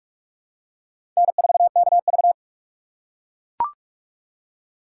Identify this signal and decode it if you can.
Morse code, heard as a keyed tone, about 44 wpm, 700 Hz. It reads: N4KV
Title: USA Call Signs - 4 Characters - 3x Character Spacing - 45wpm